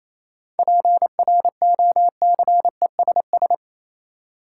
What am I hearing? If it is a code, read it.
Morse code, heard as a keyed tone, about 28 wpm, 700 Hz. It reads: PROCESS
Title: Sentences from Top 1000 Words - Review Part 3 - New Word Drill - 28wpm